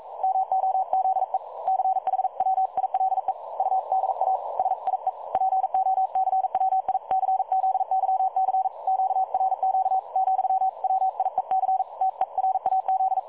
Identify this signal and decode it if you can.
Morse code, heard as a keyed tone, about 42 wpm, 760 Hz. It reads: MOZE CHWILE UDA SIE POCWICZYC QRQ, WIEC TERAZ
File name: dlazaawansowanych / nagraniazpasma / websdr recording 3565.0kHz